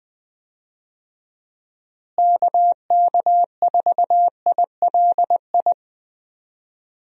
Morse code, heard as a keyed tone, about 20 wpm, 700 Hz. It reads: KK4ILI
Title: USA Call Signs - 6 Characters - Repeated 3x - 20wpm